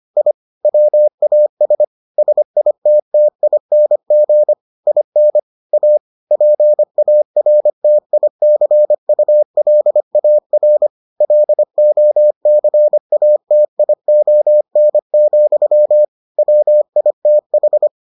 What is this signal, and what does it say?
Morse code, heard as a keyed tone, about 25 wpm, 600 Hz. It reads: I WAS SITTING IN A PARTICULAR LOCATION, WITH